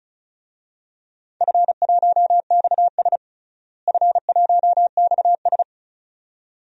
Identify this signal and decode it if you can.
Morse code, heard as a keyed tone, about 34 wpm, 700 Hz. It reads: F1XS F1XS
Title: DXCC-100 Call Signs - 4 Characters - Repeated 2x - 35wpm